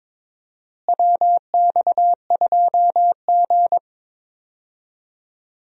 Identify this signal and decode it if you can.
Morse code, heard as a keyed tone, about 22 wpm, 700 Hz. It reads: WX2G
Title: USA Call Signs - 4 Characters - Repeated 2x - 22wpm